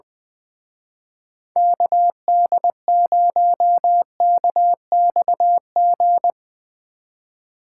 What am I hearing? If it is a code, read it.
Morse code, heard as a keyed tone, about 21 wpm, 700 Hz. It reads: KD0KXG
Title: Speed Racing - USA Call Signs 20wpm